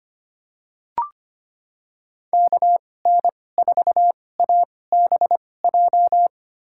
Morse code, heard as a keyed tone, about 25 wpm, spaced as for 18 wpm, 700 Hz. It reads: KN4ABJ
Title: USA Call Signs - 6 Characters - 2x Character Spacing - 25wpm